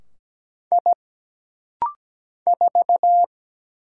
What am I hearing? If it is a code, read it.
Morse code, heard as a keyed tone, about 17 wpm, 700 Hz. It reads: I 4
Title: Single Letter-Number - Rapid-Fire 17wpm